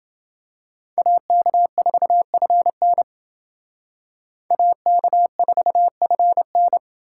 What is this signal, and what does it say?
Morse code, heard as a keyed tone, about 28 wpm, 700 Hz. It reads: AK4FN AK4FN